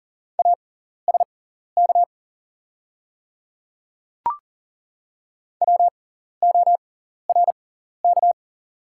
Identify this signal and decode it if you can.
Morse code, heard as a keyed tone, about 37 wpm, spaced as for 14 wpm, 700 Hz. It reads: ASK WORK
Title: 6x Character Spacing - Top 500 Words - 40wpm